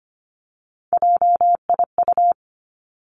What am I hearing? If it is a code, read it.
Morse code, heard as a keyed tone, about 25 wpm, 700 Hz. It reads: JIU
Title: Three Challenging Letter-Number Combinations 25wpm